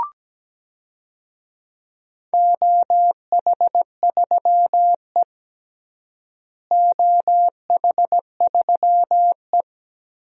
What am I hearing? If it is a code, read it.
Morse code, heard as a keyed tone, about 17 wpm, 700 Hz. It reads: OH3E OH3E
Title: DXCC-100 Call Signs - 4 Characters - Repeated 3x - 17wpm